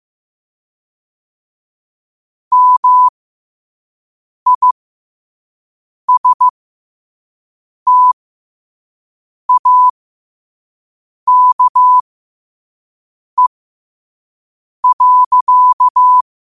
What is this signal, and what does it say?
Morse code, heard as a keyed tone, about 15 wpm, spaced as for 5 wpm, 1000 Hz. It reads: MISTAKE.
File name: qotdfiles / 5wpm